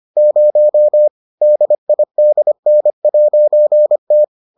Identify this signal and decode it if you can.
Morse code, heard as a keyed tone, about 25 wpm, 600 Hz. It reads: 0 DIDN'T